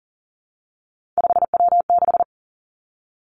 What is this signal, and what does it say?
Morse code, heard as a keyed tone, about 38 wpm, 700 Hz. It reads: 5W6
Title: Three Challenging Letter-Number Combinations 40wpm